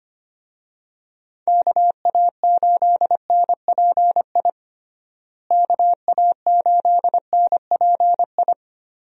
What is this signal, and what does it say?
Morse code, heard as a keyed tone, about 25 wpm, 700 Hz. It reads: KA8NPI KA8NPI
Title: USA Call Signs - 6 Characters - Repeated 2x - 25wpm